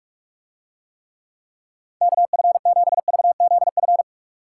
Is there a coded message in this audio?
Yes: Morse code from 2.0 to 4.0 s, heard as a keyed tone, about 44 wpm, 700 Hz. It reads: KF7VZF